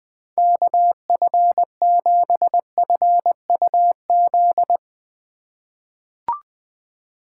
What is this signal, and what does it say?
Morse code, heard as a keyed tone, about 21 wpm, 700 Hz. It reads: KF7FUZ
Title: USA Call Signs - 6 Characters - 3x Character Spacing - 20wpm